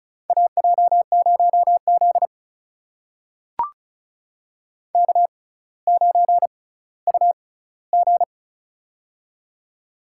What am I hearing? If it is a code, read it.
Morse code, heard as a keyed tone, about 36 wpm, spaced as for 18 wpm, 700 Hz. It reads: AJ0Z K9UG
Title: USA Call Signs - 4 Characters - 6x Character Spacing - 35wpm